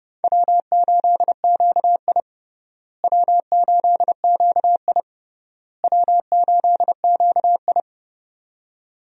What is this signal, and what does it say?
Morse code, heard as a keyed tone, about 32 wpm, 700 Hz. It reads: W8QI W8QI W8QI
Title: USA Call Signs - 4 Characters - Repeated 3x - 30wpm